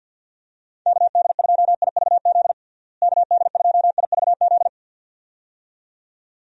Morse code, heard as a keyed tone, about 52 wpm, 700 Hz. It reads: KD2IVZ KD2IVZ
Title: USA Call Signs - 6 Characters - Repeated 2x - 50wpm